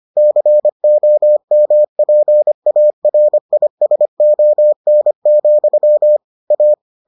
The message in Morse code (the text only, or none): COMPARISON, A